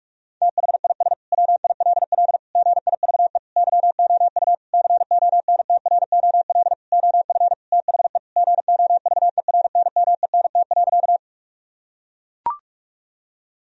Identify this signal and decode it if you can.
Morse code, heard as a keyed tone, about 43 wpm, 700 Hz. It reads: THIS WILL GIVE YOU CONTROL OF THE GOVERNMENT.
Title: Sentences from Top 1000 Words - Review Part 3 - 45wpm